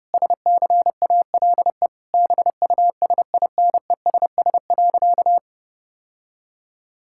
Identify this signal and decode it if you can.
Morse code, heard as a keyed tone, about 30 wpm, 700 Hz. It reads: SCALE BUSINESS.
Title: Sentences from Top 1000 Words - Review Part 5 - 30wpm